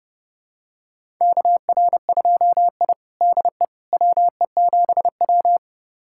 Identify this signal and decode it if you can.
Morse code, heard as a keyed tone, about 30 wpm, 700 Hz. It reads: KR2I DE WE7W
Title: USA Callsign DE Callsign - 4 Characters - 30wpm